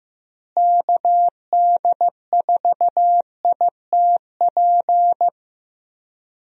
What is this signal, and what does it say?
Morse code, heard as a keyed tone, about 15 wpm, 700 Hz. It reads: KD4ITP
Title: USA Call Signs - 6 Characters - Repeated 2x - 15wpm